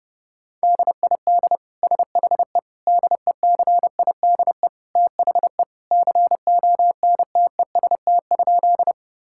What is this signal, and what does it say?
Morse code, heard as a keyed tone, about 30 wpm, 700 Hz. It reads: DID SHE DECIDE THE CONTEST?
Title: Sentences from Top 1600 Words 30wpm